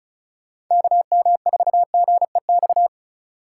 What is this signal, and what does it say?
Morse code, heard as a keyed tone, about 45 wpm, 700 Hz. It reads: KM4GEX